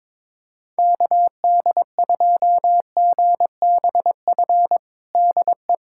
Morse code, heard as a keyed tone, about 22 wpm, 700 Hz. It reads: KD2GBF DE
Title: USA Callsign DE Callsign 22wpm